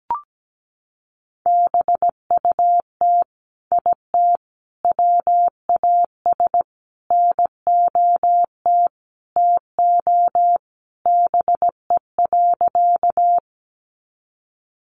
Morse code, heard as a keyed tone, about 17 wpm, 700 Hz. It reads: BUT IT WAS NOT TO BE.
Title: Sentences from Top 600 Words 17wpm